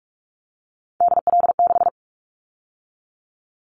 Morse code, heard as a keyed tone, about 44 wpm, 700 Hz. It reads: DL6
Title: Callsign Prefixes for Top 25 DXCC - 45wpm